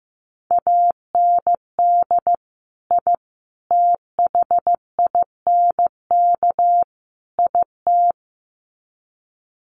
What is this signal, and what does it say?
Morse code, heard as a keyed tone, about 15 wpm, 700 Hz. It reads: AND I THINK IT
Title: Sets of 4 Words 15wpm